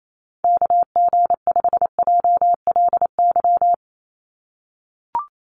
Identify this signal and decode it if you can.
Morse code, heard as a keyed tone, about 28 wpm, 700 Hz. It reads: KG5JLY
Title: USA Call Signs 28wpm